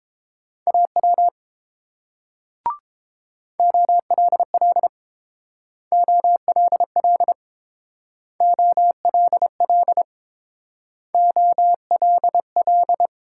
Morse code, heard as a keyed tone, about 31 wpm, 700 Hz. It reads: AW OLL OLL OLL OLL